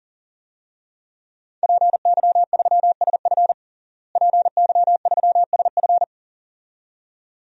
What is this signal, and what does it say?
Morse code, heard as a keyed tone, about 40 wpm, 700 Hz. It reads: PY3SF PY3SF
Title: DXCC-100 Call Signs - 5 Characters - Repeated 2x - 40wpm